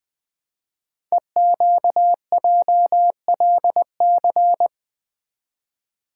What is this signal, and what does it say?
Morse code, heard as a keyed tone, about 20 wpm, 700 Hz. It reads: EQJLC
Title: Code Groups - 5 Letter:Numbers - 20wpm